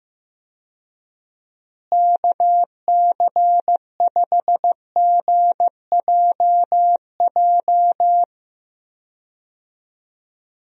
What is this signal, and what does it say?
Morse code, heard as a keyed tone, about 15 wpm, 700 Hz. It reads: KC5GJJ
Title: USA Call Signs - 6 Characters - Repeated 2x - 15wpm